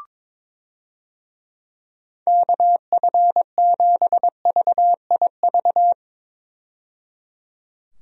Morse code, heard as a keyed tone, about 22 wpm, 700 Hz. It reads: KF7VIV
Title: USA Call Signs - 6 Characters - 22wpm